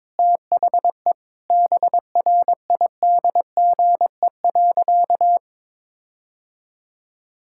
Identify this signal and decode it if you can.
Morse code, heard as a keyed tone, about 22 wpm, 700 Hz. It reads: THE BRIDGE.
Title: Sentences from Top 1600 Words 22wpm